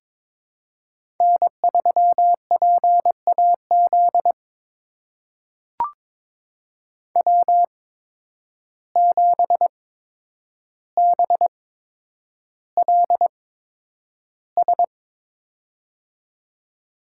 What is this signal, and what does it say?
Morse code, heard as a keyed tone, about 22 wpm, spaced as for 9 wpm, 700 Hz. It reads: N3PAZ W7BLS
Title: USA Call Signs - 5 Characters - 8x Character Spacing - 22wpm